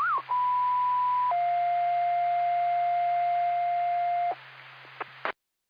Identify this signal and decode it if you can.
Two-tone page, heard as a one-second tone, then a three-second tone, about 990 then 700 Hz. It is heard from 0.3 to 4.3 s.